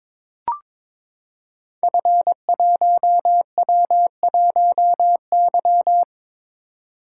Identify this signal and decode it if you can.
Morse code, heard as a keyed tone, about 23 wpm, 700 Hz. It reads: F1W1Y
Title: Code Groups - 5 Letter:Numbers - 22wpm